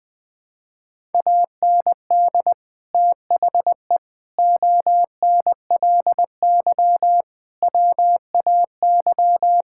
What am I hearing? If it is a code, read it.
Morse code, heard as a keyed tone, about 21 wpm, 700 Hz. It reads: AND THE ONLY WAY
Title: Sentences with 9-Word Phrases - 20wpm